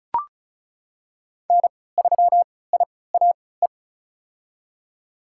Morse code, heard as a keyed tone, about 36 wpm, spaced as for 20 wpm, 700 Hz. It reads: N3IAE
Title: USA Call Signs - 5 Characters - 3x Character Spacing - 35wpm